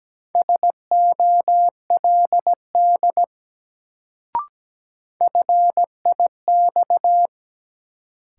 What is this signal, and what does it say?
Morse code, heard as a keyed tone, about 17 wpm, 700 Hz. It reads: SOLD FIX
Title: Sentences from Top 1400 Words - New Word Drill Alt - 17wpm